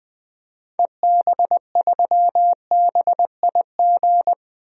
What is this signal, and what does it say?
Morse code, heard as a keyed tone, about 20 wpm, 700 Hz. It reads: EB3BIG
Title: DXCC-100 Call Signs - 6 Characters - 20wpm